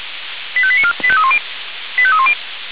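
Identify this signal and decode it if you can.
Five-tone selective call (ZVEI), heard as quick runs of five tones, tones of about 70 ms. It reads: 85894 85310 85310
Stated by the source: ZVEI five-tone selective call 85265 received by SDR.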